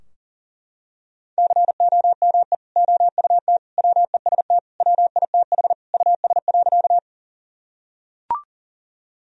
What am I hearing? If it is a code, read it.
Morse code, heard as a keyed tone, about 40 wpm, 700 Hz. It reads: COME OUT WEST WITH US.
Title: Sentences from Top 1000 Words - Review Part 2 - 40wpm